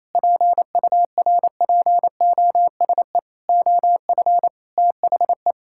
Morse code, heard as a keyed tone, about 28 wpm, 700 Hz. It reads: PURPOSE OF THE
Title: Sentences from Top 1000 Words - Review Part 6 - 28wpm